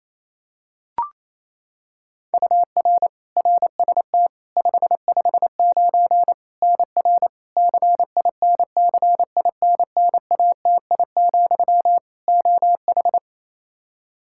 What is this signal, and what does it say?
Morse code, heard as a keyed tone, about 28 wpm, 700 Hz. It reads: UR RST 559 NR CINCINNATI, OH
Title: K1USN SST QSO Practice 28wpm